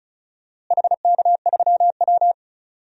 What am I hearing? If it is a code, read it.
Morse code, heard as a keyed tone, about 36 wpm, 700 Hz. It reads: HK3W
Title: DXCC-100 Call Signs - 4 Characters - Repeated 3x - 35wpm